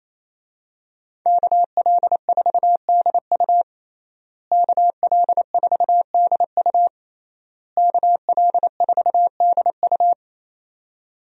Morse code, heard as a keyed tone, about 28 wpm, 700 Hz. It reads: KL4DU KL4DU KL4DU
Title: USA Call Signs - 5 Characters - Repeated 3x - 28wpm